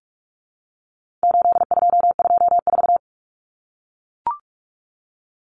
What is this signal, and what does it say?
Morse code, heard as a keyed tone, about 44 wpm, 700 Hz. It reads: 8224
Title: Common Cut Numbers - 4-Digit 45wpm